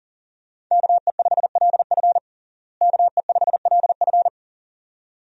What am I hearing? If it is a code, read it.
Morse code, heard as a keyed tone, about 37 wpm, 700 Hz. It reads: KE5LF KE5LF
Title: USA Call Signs - 5 Characters - Repeated 2x - 40wpm